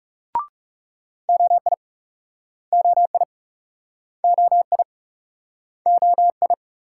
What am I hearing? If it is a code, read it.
Morse code, heard as a keyed tone, about 38 wpm, 700 Hz. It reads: OI OI OI OI